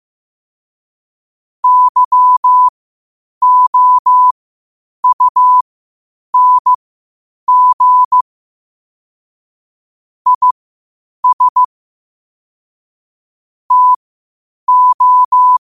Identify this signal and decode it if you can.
Morse code, heard as a keyed tone, about 15 wpm, spaced as for 8 wpm, 1000 Hz. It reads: YOUNG IS TO
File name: qotdfiles / 8wpm